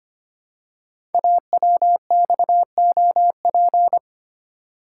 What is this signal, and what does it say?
Morse code, heard as a keyed tone, about 25 wpm, 690 Hz. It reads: AWXOP
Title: Code Groups - 5 Letters - 25wpm